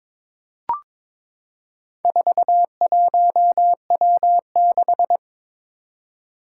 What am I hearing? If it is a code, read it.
Morse code, heard as a keyed tone, about 22 wpm, 700 Hz. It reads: 41W6
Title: Four Challenging Letter-Number Combinations 22wpm